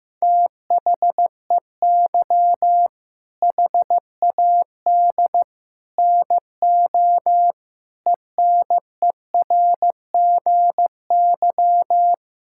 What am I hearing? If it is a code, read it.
Morse code, heard as a keyed tone, about 15 wpm, 700 Hz. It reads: THEY HAD NO ENERGY
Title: Sentences from Top 1000 Words - Review Part 6 - 15wpm